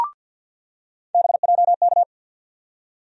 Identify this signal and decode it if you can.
Morse code, heard as a keyed tone, about 53 wpm, 700 Hz. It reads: BJK
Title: USA Call Sign Postfixes - 3 Letters 50wpm